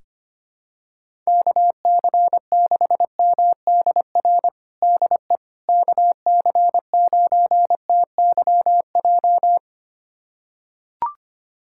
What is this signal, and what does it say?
Morse code, heard as a keyed tone, about 25 wpm, 700 Hz. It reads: KC6MDR DE KC9TYJ
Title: USA Callsign DE Callsign - 6 Characters - 25wpm